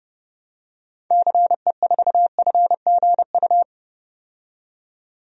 Morse code, heard as a keyed tone, about 32 wpm, 700 Hz. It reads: CE4FGU
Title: DXCC-100 Call Signs - 6 Characters - 30wpm